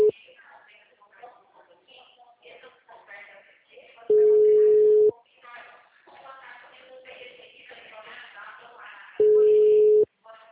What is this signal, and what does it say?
Ringback tone (Europe, 425 Hz), with speech.